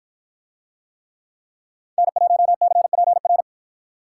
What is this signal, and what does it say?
Morse code, heard as a keyed tone, about 53 wpm, 700 Hz. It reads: N1CPR